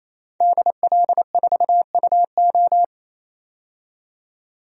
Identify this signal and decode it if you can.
Morse code, heard as a keyed tone, about 28 wpm, 700 Hz. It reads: DL4UO
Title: DXCC-100 Call Signs - 5 Characters - 28wpm